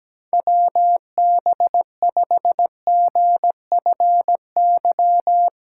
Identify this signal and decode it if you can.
Morse code, heard as a keyed tone, about 17 wpm, 700 Hz. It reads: WB5GFY
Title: USA Call Signs - 3x Character Spacing - 17wpm